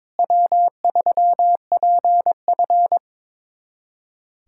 Morse code, heard as a keyed tone, about 21 wpm, 700 Hz. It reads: W3PF
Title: USA Callsign DE Callsign - 4 Characters - 22wpm